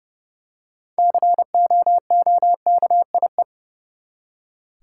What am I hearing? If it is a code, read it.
Morse code, heard as a keyed tone, about 32 wpm, 700 Hz. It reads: COOKIE
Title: Six Letter Words - 30wpm